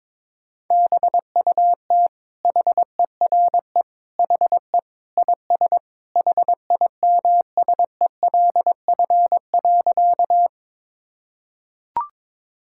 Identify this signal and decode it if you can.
Morse code, heard as a keyed tone, about 22 wpm, 700 Hz. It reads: BUT HERE HE IS HIMSELF.
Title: Sentences from Top 1600 Words 22wpm